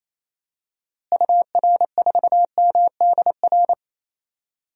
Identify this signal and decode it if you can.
Morse code, heard as a keyed tone, about 28 wpm, 700 Hz. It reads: UR4MDR
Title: DXCC-100 Call Signs - 6 Characters - Repeated 3x - 28wpm